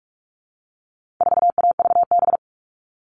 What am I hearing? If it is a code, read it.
Morse code, heard as a keyed tone, about 44 wpm, 700 Hz. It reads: 4AVB
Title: All Cut Numbers - 4-Digit 45wpm